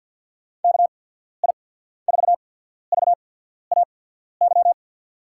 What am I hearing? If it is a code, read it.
Morse code, heard as a keyed tone, about 53 wpm, spaced as for 14 wpm, 700 Hz. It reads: KI4VAY
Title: USA Call Signs - 6 Characters - 8x Character Spacing - 50wpm